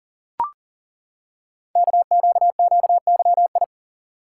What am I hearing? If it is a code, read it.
Morse code, heard as a keyed tone, about 40 wpm, 690 Hz. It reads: KQQYI
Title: Code Groups - 5 Letter:Numbers - 40wpm